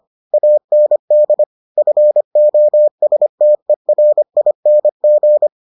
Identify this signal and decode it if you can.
Morse code, heard as a keyed tone, about 25 wpm, 600 Hz. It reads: AND FOSTERING